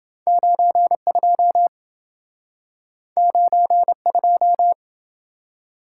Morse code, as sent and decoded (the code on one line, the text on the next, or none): ----. ..--- / ----. ..---
92 92